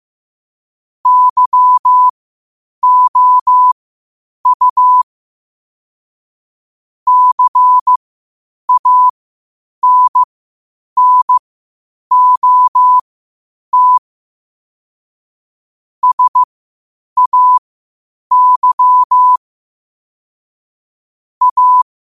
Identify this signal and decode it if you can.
Morse code, heard as a keyed tone, about 15 wpm, spaced as for 8 wpm, 1000 Hz. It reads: YOU CANNOT SAY A
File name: qotdfiles / 8wpm